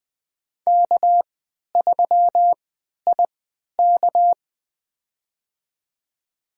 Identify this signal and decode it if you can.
Morse code, heard as a keyed tone, about 20 wpm, spaced as for 11 wpm, 700 Hz. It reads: K3IK
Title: USA Call Signs - 4 Characters - 3x Character Spacing - 20wpm